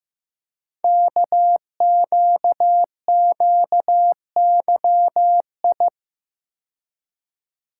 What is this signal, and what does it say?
Morse code, heard as a keyed tone, about 15 wpm, 700 Hz. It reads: KQQYI
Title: Code Groups - 5 Letter:Numbers - 15wpm